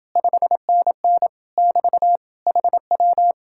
Morse code, heard as a keyed tone, about 28 wpm, 700 Hz. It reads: 5NN <BT> HW